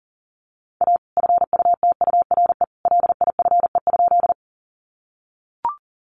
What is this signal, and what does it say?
Morse code, heard as a keyed tone, about 40 wpm, 700 Hz. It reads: A FUTURE LIFE?
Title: Sentences from Top 700 Words 40wpm